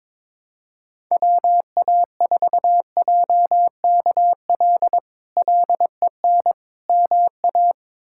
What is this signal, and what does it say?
Morse code, heard as a keyed tone, about 21 wpm, 700 Hz. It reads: WA4JKL LEN MA
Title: K1USN SST QSO Practice - 22wpm